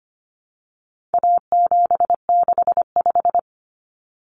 Morse code, heard as a keyed tone, about 25 wpm, 700 Hz. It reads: A765
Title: Common Cut Numbers - 4-Digit 25wpm